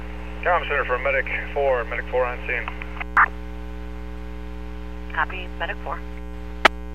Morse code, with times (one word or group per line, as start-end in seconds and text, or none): none